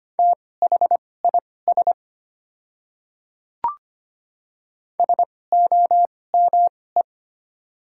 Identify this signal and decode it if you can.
Morse code, heard as a keyed tone, about 25 wpm, spaced as for 18 wpm, 690 Hz. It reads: THIS SOME
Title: 2x Character Spacing - Top 100 Words - 25wpm